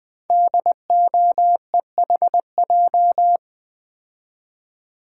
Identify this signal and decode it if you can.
Morse code, heard as a keyed tone, about 20 wpm, 700 Hz. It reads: DOEHJ
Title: Code Groups - 5 Letters - 20wpm